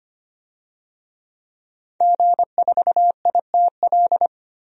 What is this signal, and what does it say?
Morse code, heard as a keyed tone, about 25 wpm, 700 Hz. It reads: G4ITL
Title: DXCC-100 Call Signs - 5 Characters - 25wpm